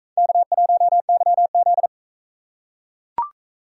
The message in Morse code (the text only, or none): K1YZ